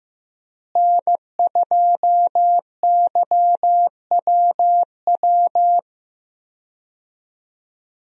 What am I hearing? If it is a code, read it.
Morse code, heard as a keyed tone, about 15 wpm, 700 Hz. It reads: N2YWW
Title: USA Call Signs - 5 Characters - Repeated 2x - 15wpm